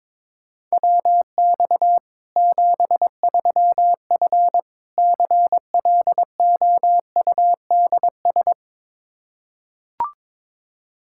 Morse code, heard as a keyed tone, about 22 wpm, 700 Hz. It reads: WX 73F CLOUDS
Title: Practice QSO - Exchange 2 - Info - 22wpm